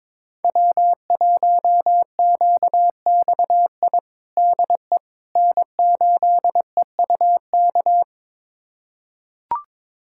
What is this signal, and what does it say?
Morse code, heard as a keyed tone, about 22 wpm, 700 Hz. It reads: W1QXI DE N8EUK
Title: USA Callsign DE Callsign - 5 Characters - 22wpm